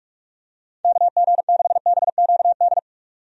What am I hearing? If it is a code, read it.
Morse code, heard as a keyed tone, about 42 wpm, 700 Hz. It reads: KG6BQD
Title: USA Call Signs - 6 Characters - 45wpm